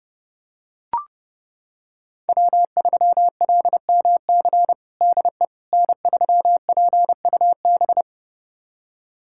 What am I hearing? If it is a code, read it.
Morse code, heard as a keyed tone, about 30 wpm, 700 Hz. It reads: W3LMC DE N3PUB
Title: USA Callsign DE Callsign - 5 Characters - 30wpm